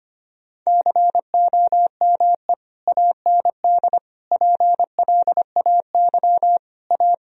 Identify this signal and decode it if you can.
Morse code, heard as a keyed tone, about 25 wpm, 700 Hz. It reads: COME AND PLAY A
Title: Sentences from Top 1000 Words - Review Part 4 - 25wpm